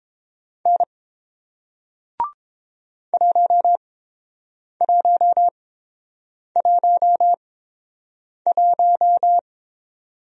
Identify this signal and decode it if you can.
Morse code, heard as a keyed tone, about 30 wpm, spaced as for 7 wpm, 700 Hz. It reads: N 1111